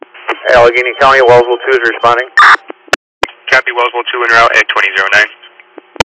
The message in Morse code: none

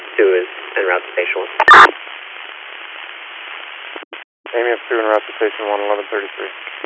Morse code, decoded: none

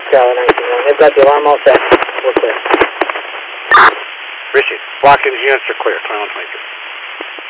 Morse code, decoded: none

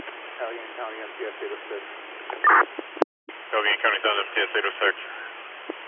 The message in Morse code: none

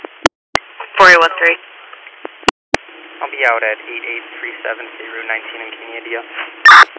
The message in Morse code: none

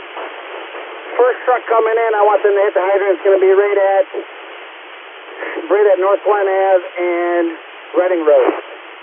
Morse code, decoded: none

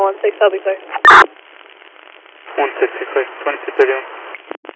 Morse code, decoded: none

